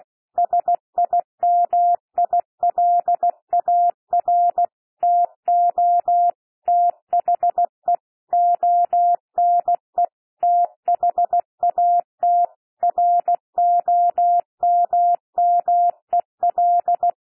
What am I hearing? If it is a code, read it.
Morse code, heard as a keyed tone, about 16 wpm, 700 Hz. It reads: SIMILAR TO THE ONE THAT ROMMEL
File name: cwmp3 / P1 SOUND PASSAGE 16 wpm